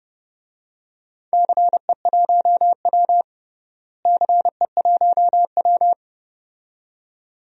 Morse code, heard as a keyed tone, about 30 wpm, 700 Hz. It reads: CE1W CE1W
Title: DXCC-100 Call Signs - 4 Characters - Repeated 2x - 30wpm